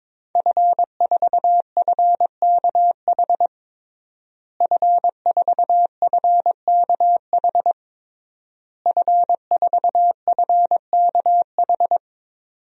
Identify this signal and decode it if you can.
Morse code, heard as a keyed tone, about 22 wpm, 700 Hz. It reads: F4FKH F4FKH F4FKH
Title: DXCC-100 Call Signs - 5 Characters - Repeated 3x - 22wpm